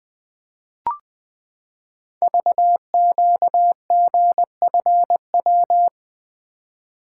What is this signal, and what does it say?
Morse code, heard as a keyed tone, about 20 wpm, 700 Hz. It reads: VQGFW
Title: Code Groups - 5 Letters - 20wpm